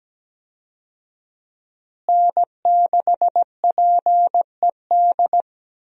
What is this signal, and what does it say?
Morse code, heard as a keyed tone, about 17 wpm, 700 Hz. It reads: N6PED
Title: USA Call Signs - 5 Characters - Repeated 2x - 17wpm